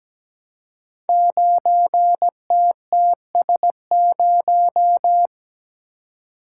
Morse code, heard as a keyed tone, about 17 wpm, 700 Hz. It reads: 9TTS0
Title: Code Groups - 5 Letter:Numbers - 17wpm